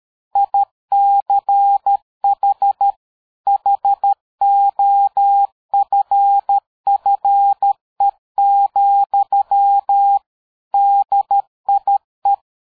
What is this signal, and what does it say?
Morse code, heard as a keyed tone, about 13 wpm, 800 Hz. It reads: ICH HOFFE, DIE
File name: dk5ke / PersWorte 60